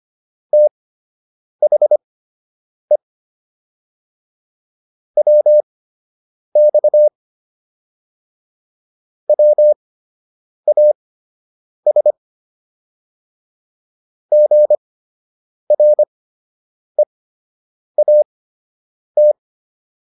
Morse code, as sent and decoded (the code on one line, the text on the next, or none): - .... / .-- -..- / .-- .- ... / --. .-. .- -
THE WX WAS GREAT